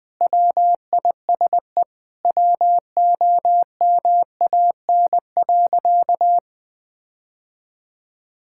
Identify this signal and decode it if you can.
Morse code, heard as a keyed tone, about 20 wpm, 700 Hz. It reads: WISE WOMAN.
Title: Sentences from Top 1600 Words 20wpm